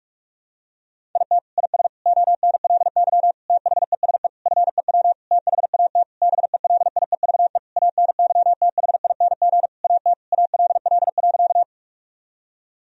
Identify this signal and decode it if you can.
Morse code, heard as a keyed tone, about 46 wpm, 700 Hz. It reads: IT IS ONLY THESE FEW THAT BELIEVE ANYTHING AT ALL.
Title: Sentences from Top 1000 Words - Review Part 1 - 45wpm